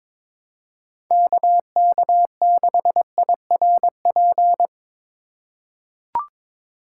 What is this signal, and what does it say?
Morse code, heard as a keyed tone, about 22 wpm, 700 Hz. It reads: KK6IRP
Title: USA Call Signs - 3x Character Spacing - 22wpm